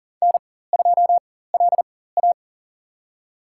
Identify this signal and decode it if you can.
Morse code, heard as a keyed tone, about 40 wpm, spaced as for 19 wpm, 700 Hz. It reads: N2LA
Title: USA Call Signs - 4x Character Spacing - 40wpm